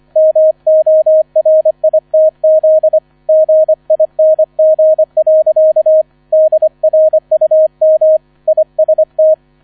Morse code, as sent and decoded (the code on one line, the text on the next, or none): -- --- .-. .. - --.. / --. .. -. --. .-.-.- / -.. .-. ..- -- / .. ... -
MORITZ GING. DRUM IST